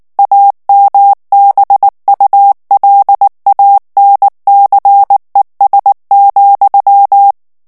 Morse code, heard as a keyed tone, about 19 wpm, 800 Hz. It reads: AMBULANCES,
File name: morse mp3 / ANGLAIS texte clair 2 vitesse 18 mots minute